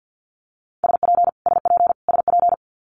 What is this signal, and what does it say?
Morse code, heard as a keyed tone, about 50 wpm, 700 Hz. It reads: SP SP SP
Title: Multi-letter Phonemes Repeated 3x 50wpm